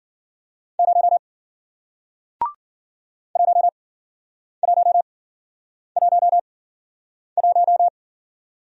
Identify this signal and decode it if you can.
Morse code, heard as a keyed tone, about 53 wpm, spaced as for 9 wpm, 700 Hz. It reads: 0 1111